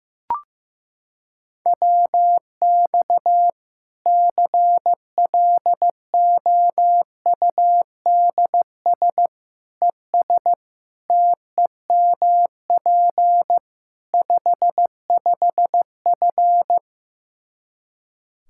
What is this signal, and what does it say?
Morse code, heard as a keyed tone, about 15 wpm, 700 Hz. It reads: WX CLOUDS ES TEMP 55F
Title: Practice QSO - Exchange 2 - Info - 15wpm